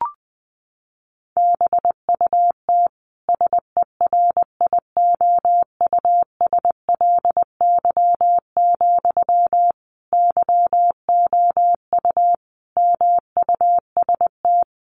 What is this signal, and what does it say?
Morse code, heard as a keyed tone, about 20 wpm, 700 Hz. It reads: BUT SERIOUSLY, YOU MUST
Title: Sentences from Top 1200 Words 20wpm